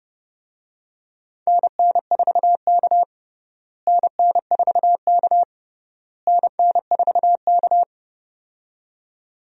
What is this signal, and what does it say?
Morse code, heard as a keyed tone, about 30 wpm, 700 Hz. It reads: NN4K NN4K NN4K
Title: USA Call Signs - 4 Characters - Repeated 3x - 30wpm